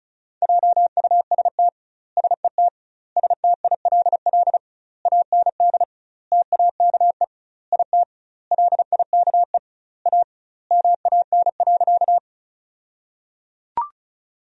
Morse code, heard as a keyed tone, about 36 wpm, 700 Hz. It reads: JUST SET STILL AND TAKE IT LIKE A MAN.
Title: 2x Word-Spacing - Sentences from Top 300 Words - 35wpm